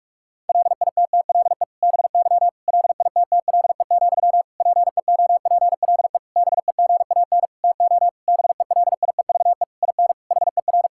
Morse code, heard as a keyed tone, about 46 wpm, 700 Hz. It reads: LITTLE BY LITTLE, PEOPLE BEGAN TO BELIEVE IN HER
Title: Sentences from Top 1600 Words 45wpm